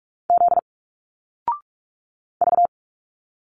Morse code, heard as a keyed tone, about 46 wpm, 700 Hz. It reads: Z V
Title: Single Letters 45wpm